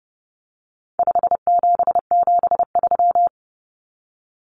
Morse code, heard as a keyed tone, about 30 wpm, 700 Hz. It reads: 5773
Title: Common Cut Numbers - 4-Digit 30wpm